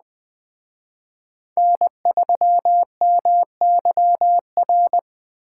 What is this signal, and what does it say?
Morse code, heard as a keyed tone, about 19 wpm, 700 Hz. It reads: N3MYR